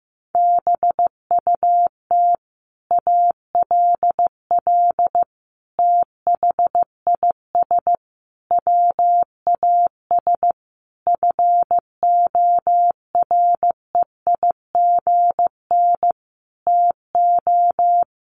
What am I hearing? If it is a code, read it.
Morse code, heard as a keyed tone, about 15 wpm, 700 Hz. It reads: BUT ALL THIS WAS FOREIGN TO
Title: Sentences from Top 1400 Words 15wpm